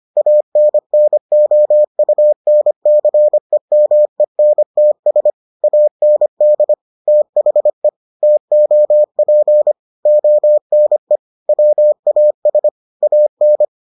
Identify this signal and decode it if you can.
Morse code, heard as a keyed tone, about 25 wpm, 600 Hz. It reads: ANNOUNCEMENTS AND THE TOP ONE WAS AN